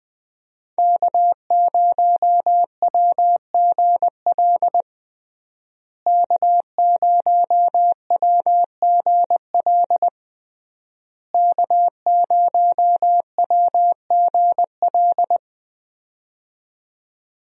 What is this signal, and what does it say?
Morse code, heard as a keyed tone, about 20 wpm, 700 Hz. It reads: K0WGL K0WGL K0WGL
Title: USA Call Signs - 5 Characters - Repeated 3x - 20wpm